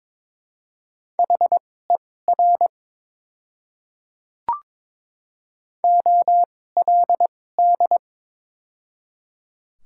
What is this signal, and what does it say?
Morse code, heard as a keyed tone, about 22 wpm, spaced as for 16 wpm, 700 Hz. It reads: HER OLD
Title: 2x Character Spacing - Top 300 Words - 22wpm